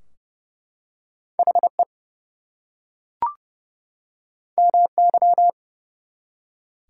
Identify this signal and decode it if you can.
Morse code, heard as a keyed tone, about 28 wpm, 700 Hz. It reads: HE MY
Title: Two Letter Words - 30wpm